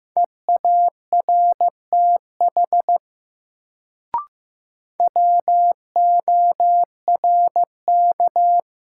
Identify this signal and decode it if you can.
Morse code, heard as a keyed tone, about 15 wpm, 700 Hz. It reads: EARTH WORK
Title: Sentences from Top 200 Words - New Word Drill - 15wpm